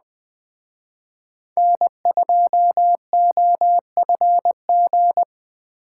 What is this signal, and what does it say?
Morse code, heard as a keyed tone, about 21 wpm, 700 Hz. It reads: N2OFG